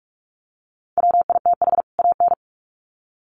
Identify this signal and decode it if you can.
Morse code, heard as a keyed tone, about 44 wpm, 700 Hz. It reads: WITH AN
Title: Sets of 2 Words 45wpm